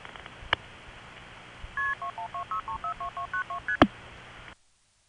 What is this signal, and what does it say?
Multi-frequency trunk signalling, heard as fast pairs of tones, tones of about 110 ms. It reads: KP2126342292ST
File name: phonetrips / RamseurNC2HQ